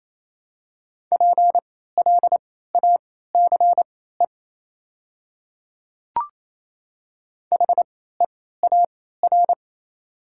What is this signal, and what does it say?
Morse code, heard as a keyed tone, about 28 wpm, spaced as for 16 wpm, 700 Hz. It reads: PLACE HEAR
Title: 3x Character Spacing - Top 500 Words - 28wpm